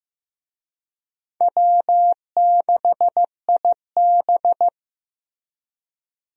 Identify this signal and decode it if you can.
Morse code, heard as a keyed tone, about 15 wpm, 700 Hz. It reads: W6IB
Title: USA Call Signs - 4 Characters - Repeated 3x - 15wpm